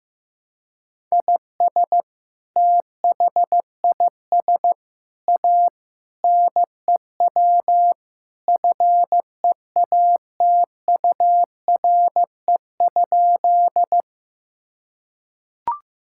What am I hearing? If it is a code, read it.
Morse code, heard as a keyed tone, about 15 wpm, 700 Hz. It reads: IS THIS A NEW FEATURE?
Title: Sentences from Top 1000 Words - Review Part 5 - 15wpm